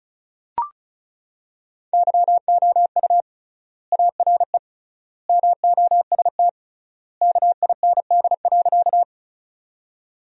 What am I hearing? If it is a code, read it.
Morse code, heard as a keyed tone, about 34 wpm, 700 Hz. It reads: YOU ARE MOST KIND.
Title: 3x Word-Spacing - Sentences from Top 200 Words - 35wpm